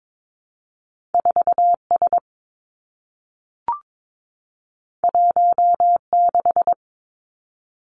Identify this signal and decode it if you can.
Morse code, heard as a keyed tone, about 22 wpm, 700 Hz. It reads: 4 S 1 6
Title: Two Challenging Letter-Number Combinations 22wpm